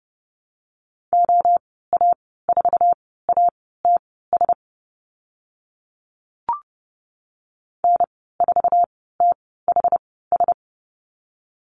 Morse code, heard as a keyed tone, about 30 wpm, spaced as for 17 wpm, 700 Hz. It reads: OA4ATS N4THS
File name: Lesson 14g - Number 2 - Call Signs 30wpm